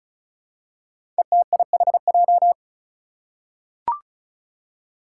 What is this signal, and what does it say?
Morse code, heard as a keyed tone, about 36 wpm, 700 Hz. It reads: ETIHJ
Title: Five Challenging Letter-Number Combinations 35wpm